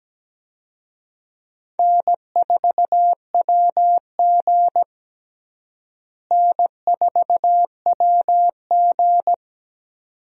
Morse code, heard as a keyed tone, about 17 wpm, 700 Hz. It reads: N4WG N4WG
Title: USA Call Signs - 4 Characters - Repeated 2x - 17wpm